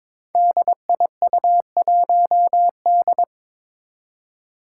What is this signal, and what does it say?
Morse code, heard as a keyed tone, about 22 wpm, 700 Hz. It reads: DIU1D
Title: Five Challenging Letter-Number Combinations 22wpm